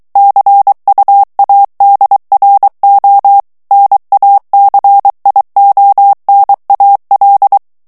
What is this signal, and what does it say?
Morse code, heard as a keyed tone, about 23 wpm, 800 Hz. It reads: CUADRO NACIONAL
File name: morse mp3 / ESPAGNOL TEXTE EN CLAIR 2 VITESSE 22 MOTS MINUTES